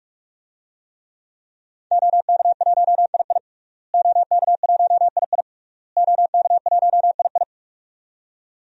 Morse code, heard as a keyed tone, about 46 wpm, 700 Hz. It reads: OK1II OK1II OK1II
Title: DXCC-100 Call Signs - 5 Characters - Repeated 3x - 45wpm